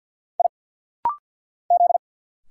Morse code, heard as a keyed tone, about 50 wpm, 700 Hz. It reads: I Z
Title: Single Letters - Rapid-Fire 50wpm